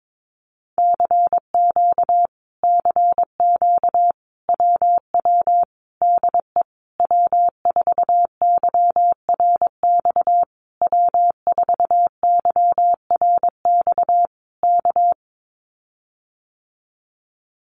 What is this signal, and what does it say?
Morse code, heard as a keyed tone, about 22 wpm, 700 Hz. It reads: CQ CQ WW DE W4YRX W4YRX K
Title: Common QSO Phrases Encore 22wpm